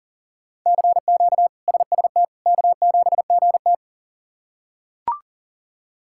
Morse code, heard as a keyed tone, about 40 wpm, 700 Hz. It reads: CQ SST K7GT
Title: K1USN SST QSO Practice - 40wpm